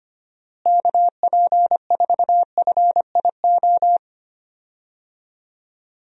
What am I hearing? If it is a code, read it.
Morse code, heard as a keyed tone, about 25 wpm, 700 Hz. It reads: KP4FIO
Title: USA Call Signs - 6 Characters - Repeated 3x - 25wpm